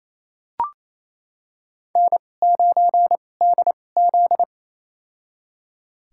Morse code, heard as a keyed tone, about 28 wpm, spaced as for 20 wpm, 700 Hz. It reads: N9DZ
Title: USA Call Signs - 4 Characters - 2x Character Spacing - 28wpm